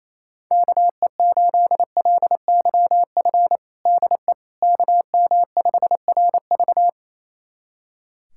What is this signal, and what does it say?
Morse code, heard as a keyed tone, about 28 wpm, 700 Hz. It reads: KE8LYF DE KM5RV
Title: USA Callsign DE Callsign 28wpm